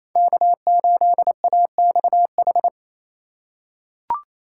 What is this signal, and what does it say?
Morse code, heard as a keyed tone, about 28 wpm, 700 Hz. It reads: K8AXH
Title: USA Call Signs - 5 Characters - 2x Character Spacing - 28wpm